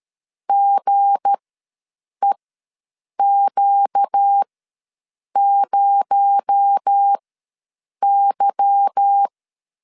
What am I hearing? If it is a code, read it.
Morse code, heard as a keyed tone, about 13 wpm, spaced as for 7 wpm, 800 Hz. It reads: GEQ0Y
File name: LEZIONI / 10 SIGNS / 10B INT ws 12wpm 3527